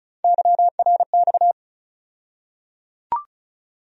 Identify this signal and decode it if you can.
Morse code, heard as a keyed tone, about 34 wpm, 700 Hz. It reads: YRX
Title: USA Call Sign Postfixes - 3 Letters 35wpm